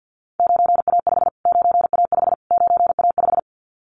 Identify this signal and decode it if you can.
Morse code, heard as a keyed tone, about 50 wpm, 700 Hz. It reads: 9A5 9A5 9A5
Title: Callsign Prefixes for Top 25 DXCC - Repeated 3x - 50wpm